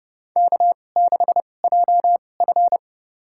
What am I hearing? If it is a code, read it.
Morse code, heard as a keyed tone, about 30 wpm, spaced as for 22 wpm, 690 Hz. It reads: K6JF
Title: USA Call Signs - 4 Characters - 2x Character Spacing - 30wpm